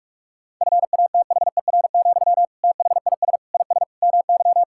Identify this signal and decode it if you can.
Morse code, heard as a keyed tone, about 46 wpm, 700 Hz. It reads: FATHER, THIS IS MY
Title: Sentences from Top 1000 Words - Review Part 6 - 45wpm